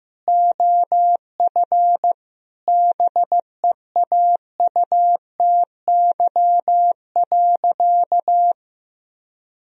Morse code, heard as a keyed tone, about 15 wpm, 700 Hz. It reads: OF BEAUTY.
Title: Sentences from Top 1000 Words - Review Part 2 - 15wpm